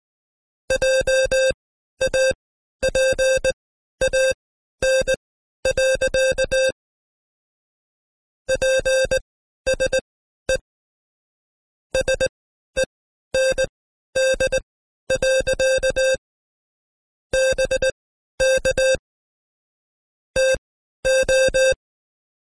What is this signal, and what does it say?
Morse code, heard as a keyed tone, about 19 wpm, spaced as for 12 wpm, 530 Hz. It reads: JAPAN. PSE SEND. BK TO
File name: Code Quick / AudioDownld2 / Seg132-Stepper5 11WPM